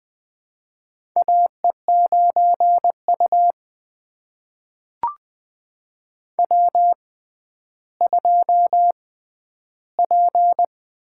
Morse code, heard as a keyed tone, about 20 wpm, spaced as for 11 wpm, 700 Hz. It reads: AE9U W2P
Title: USA Call Signs - 4 Characters - 6x Character Spacing - 20wpm